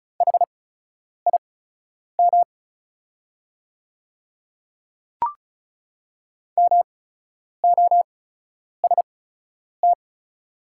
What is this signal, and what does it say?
Morse code, heard as a keyed tone, about 36 wpm, spaced as for 10 wpm, 690 Hz. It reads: HIM MOST
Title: 8x Character Spacing - Top 100 Words - 35wpm